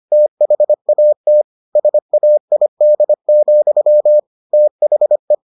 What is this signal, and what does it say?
Morse code, heard as a keyed tone, about 25 wpm, 600 Hz. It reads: THAT SAID, THE